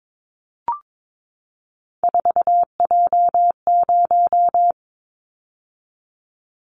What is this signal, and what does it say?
Morse code, heard as a keyed tone, about 22 wpm, 700 Hz. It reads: 4J0
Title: Callsign Prefixes for Top 100 DXCC - 22wpm